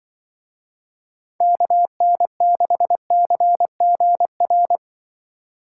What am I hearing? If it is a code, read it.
Morse code, heard as a keyed tone, about 24 wpm, 700 Hz. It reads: KN6CGR
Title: Speed Racing - USA 2x3 Call Signs 20wpm